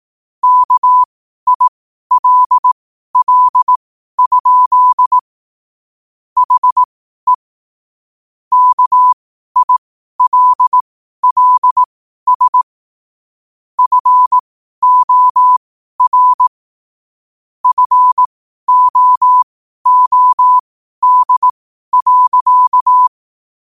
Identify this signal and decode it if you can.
Morse code, heard as a keyed tone, about 18 wpm, spaced as for 12 wpm, 1000 Hz. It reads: KILL? HE KILLS FOR FOOD.
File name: qotdfiles / 12wpm